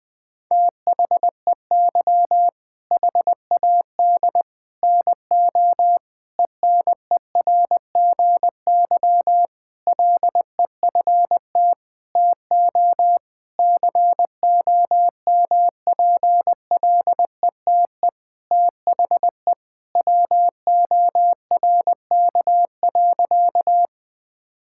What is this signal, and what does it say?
Morse code, heard as a keyed tone, about 20 wpm, 700 Hz. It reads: THEY HAD NO ENERGY LEFT TO COMPLETE THE WORK.
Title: Sentences from Top 1000 Words - Review Part 6 - 20wpm